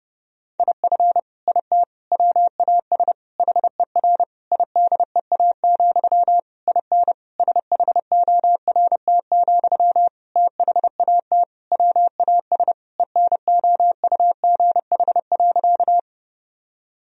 Morse code, heard as a keyed tone, about 30 wpm, 700 Hz. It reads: IF IT WAS HER IDEA, IN SHORT, THAT WAS ENOUGH.
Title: Sentences from Top 1000 Words - Review Part 2 - 30wpm